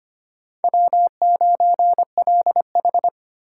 Morse code, heard as a keyed tone, about 25 wpm, 700 Hz. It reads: W9LH
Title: USA Call Signs - 4 Characters - 8x Character Spacing - 25wpm